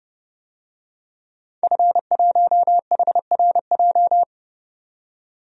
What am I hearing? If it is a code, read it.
Morse code, heard as a keyed tone, about 30 wpm, 700 Hz. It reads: F1HRJ
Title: DXCC-100 Call Signs - 5 Characters - 30wpm